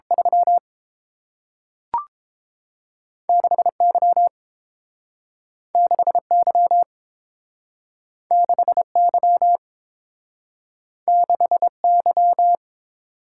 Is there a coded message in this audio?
Yes: Morse code from 0.1 to 12.6 s, heard as a keyed tone, about 30 wpm, 700 Hz. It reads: 3 6 Y 6 Y 6 Y 6 Y